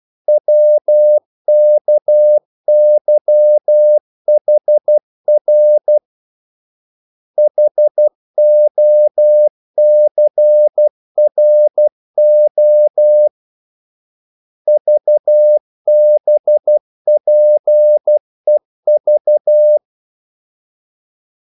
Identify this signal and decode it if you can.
Morse code, heard as a keyed tone, about 12 wpm, 600 Hz. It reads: WKYHR HOCRO VBPEV
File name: code-exemple